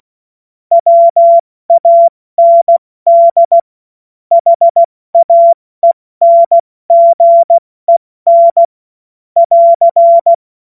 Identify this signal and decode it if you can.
Morse code, heard as a keyed tone, about 16 wpm, 680 Hz. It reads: WAND HAENGEN <AR>
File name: Morsekurs / Witze / Witz 28. September 2021, 70 F80